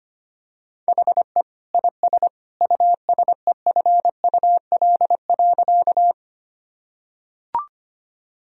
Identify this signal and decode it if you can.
Morse code, heard as a keyed tone, about 25 wpm, 700 Hz. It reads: HE IS USEFUL.
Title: Sentences from Top 1000 Words - Review Part 4 - 25wpm